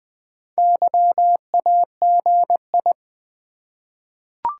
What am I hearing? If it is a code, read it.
Morse code, heard as a keyed tone, about 20 wpm, 700 Hz. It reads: YAGI
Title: Top 180 QSO Elements 20wpm